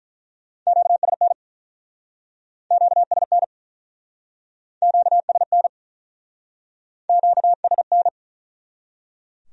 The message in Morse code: QSN QSN QSN QSN

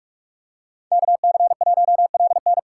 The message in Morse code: KC1LN

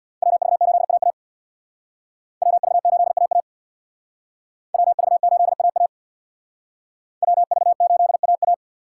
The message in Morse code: WV8AA WV8AA WV8AA WV8AA

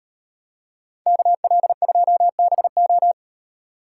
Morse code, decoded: KL2BO

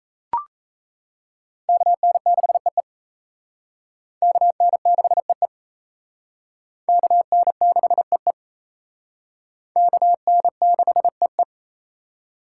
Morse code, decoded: KN6EE KN6EE KN6EE KN6EE

